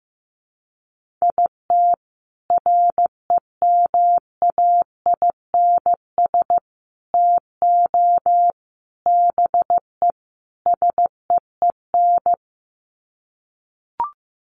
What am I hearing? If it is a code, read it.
Morse code, heard as a keyed tone, about 15 wpm, 700 Hz. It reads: IT REMAINS TO BE SEEN
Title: Sets of 5 Words Encore 15wpm